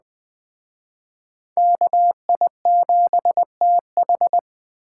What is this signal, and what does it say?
Morse code, heard as a keyed tone, about 20 wpm, 700 Hz. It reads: KI7TH